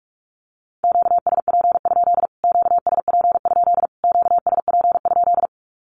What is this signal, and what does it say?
Morse code, heard as a keyed tone, about 44 wpm, 700 Hz. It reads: QSP? QSP? QSP?
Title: Q-Codes Repeated 3x 45wpm